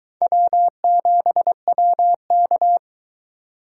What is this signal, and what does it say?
Morse code, heard as a keyed tone, about 24 wpm, 700 Hz. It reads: W7WK